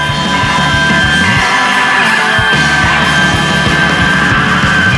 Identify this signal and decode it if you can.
Emergency Alert System (SAME) tones, heard as harsh buzzing bursts. It is heard from 0.7 to 4.4 s.